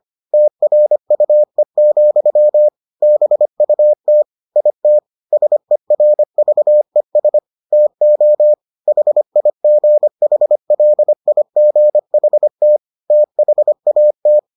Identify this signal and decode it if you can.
Morse code, heard as a keyed tone, about 25 wpm, 600 Hz. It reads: TRUE, BUT IT SERVES TO HIGHLIGHT THAT